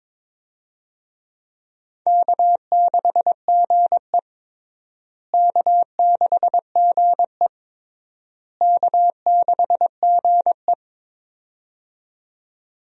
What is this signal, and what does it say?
Morse code, heard as a keyed tone, about 21 wpm, 700 Hz. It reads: K6GE K6GE K6GE
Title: USA Call Signs - 4 Characters - Repeated 3x - 22wpm